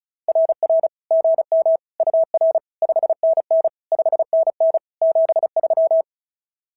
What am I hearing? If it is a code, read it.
Morse code, heard as a keyed tone, about 36 wpm, 650 Hz. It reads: RR GM UR 5NN 5NN 73
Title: SOTA QSO Practice 35wpm